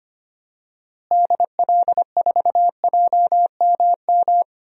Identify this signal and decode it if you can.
Morse code, heard as a keyed tone, about 25 wpm, 700 Hz. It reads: DL4JMM
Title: DXCC-100 Call Signs - 6 Characters - Repeated 2x - 25wpm